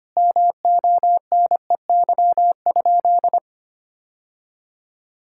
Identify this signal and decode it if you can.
Morse code, heard as a keyed tone, about 25 wpm, 700 Hz. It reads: MONEY?
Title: Sentences from Top 1000 Words - Review Part 5 - 25wpm